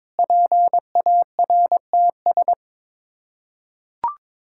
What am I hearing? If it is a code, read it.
Morse code, heard as a keyed tone, about 22 wpm, 700 Hz. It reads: PARTS
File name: practice / 22wpm / Sets of 6 Words - 22wpm